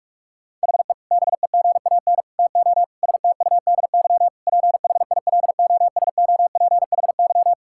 Morse code, heard as a keyed tone, about 41 wpm, 700 Hz. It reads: HE BEGAN TO STUDY PHILOSOPHY
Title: Sentences from Top 1000 Words - Review Part 6 - 45wpm